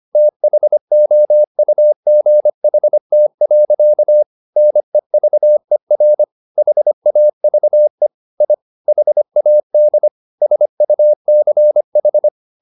Morse code, heard as a keyed tone, about 25 wpm, 600 Hz. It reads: THOUGHT. NEVER HAVE I HAD SUCH
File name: the old man and the sea part 1 25wpm